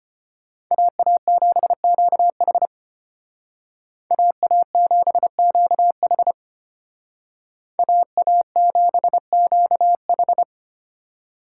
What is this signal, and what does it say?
Morse code, heard as a keyed tone, about 32 wpm, 700 Hz. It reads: AA7QH AA7QH AA7QH